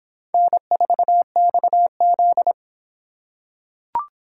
Morse code, heard as a keyed tone, about 26 wpm, 700 Hz. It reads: N4XZ